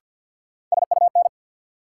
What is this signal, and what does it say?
Morse code, heard as a keyed tone, about 50 wpm, 700 Hz. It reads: SUN
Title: Sentences from Top 300 Words - New Word Drill - 50wpm